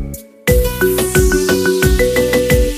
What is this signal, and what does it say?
There is no code.